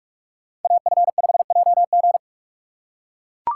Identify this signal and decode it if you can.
Morse code, heard as a keyed tone, about 39 wpm, 700 Hz. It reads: AF5JG